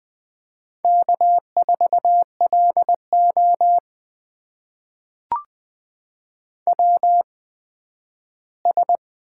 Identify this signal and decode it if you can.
Morse code, heard as a keyed tone, about 20 wpm, spaced as for 12 wpm, 700 Hz. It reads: K4LO WS
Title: USA Call Signs - 4 Characters - 8x Character Spacing - 20wpm